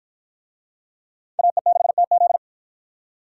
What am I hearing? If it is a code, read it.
Morse code, heard as a keyed tone, about 53 wpm, 700 Hz. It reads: AE6TZ